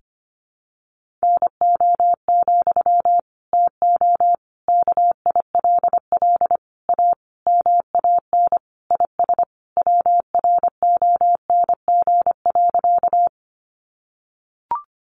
Morse code, heard as a keyed tone, about 25 wpm, 700 Hz. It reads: NO, TO KILL A MAN IS WRONG.
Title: Sentences from Top 1400 Words 25wpm